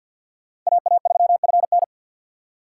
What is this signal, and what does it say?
Morse code, heard as a keyed tone, about 46 wpm, 700 Hz. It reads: AA3FN
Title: USA Call Signs - 5 Characters - 6x Character Spacing - 50wpm